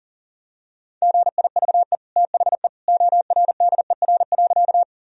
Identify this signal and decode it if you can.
Morse code, heard as a keyed tone, about 40 wpm, 700 Hz. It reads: GIVE THE ORDER.
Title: Sentences from Top 1000 Words - Review Part 2 - 40wpm